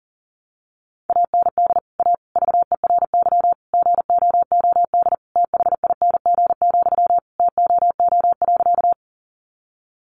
Morse code, heard as a keyed tone, about 40 wpm, 700 Hz. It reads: AND A VERY GOOD THING, TOO.
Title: Sentences from Top 200 Words 40wpm